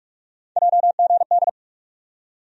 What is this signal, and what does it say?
Morse code, heard as a keyed tone, about 47 wpm, 700 Hz. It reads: JGD
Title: USA Call Sign Postfixes - 3 Letters 45wpm